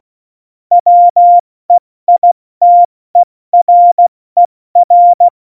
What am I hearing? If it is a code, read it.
Morse code, heard as a keyed tone, about 16 wpm, spaced as for 14 wpm, 700 Hz. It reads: WEITERER
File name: Morsekurs / Witze / Witz 13. Oktober 2020, 70 F80